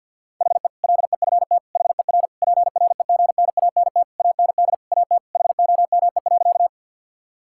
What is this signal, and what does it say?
Morse code, heard as a keyed tone, about 50 wpm, 700 Hz. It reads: HE LEFT HER PREGNANT AND AT HOME.
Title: Sentences from Top 1000 Words - Review Part 6 - 50wpm